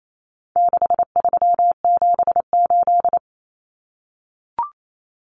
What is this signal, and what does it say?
Morse code, heard as a keyed tone, about 28 wpm, 700 Hz. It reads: 6378
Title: Numbers - 4-Digit 28wpm